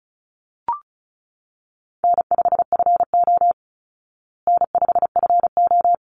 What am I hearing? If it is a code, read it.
Morse code, heard as a keyed tone, about 34 wpm, 700 Hz. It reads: N5FO N5FO
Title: Easy Call Signs Repeated 2x - 35wpm